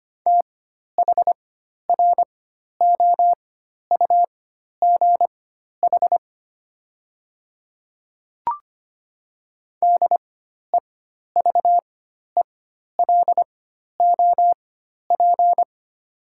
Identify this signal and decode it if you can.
Morse code, heard as a keyed tone, about 25 wpm, spaced as for 12 wpm, 700 Hz. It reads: THROUGH DEVELOP
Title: 4x Character Spacing - Top 500 Words - 25wpm